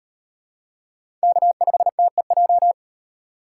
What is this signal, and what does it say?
Morse code, heard as a keyed tone, about 38 wpm, 700 Hz. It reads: K5TEJ